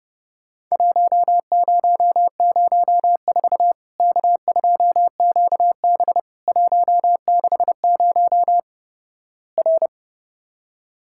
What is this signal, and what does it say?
Morse code, heard as a keyed tone, about 30 wpm, 700 Hz. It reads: 1004 K2QB 160 R
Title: Worked All Europe DX Contest - QTC Exchange - 30wpm